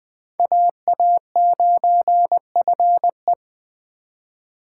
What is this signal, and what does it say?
Morse code, heard as a keyed tone, about 21 wpm, 700 Hz. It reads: AA9FE
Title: Speed Racing - USA 2x2 Call Signs 17wpm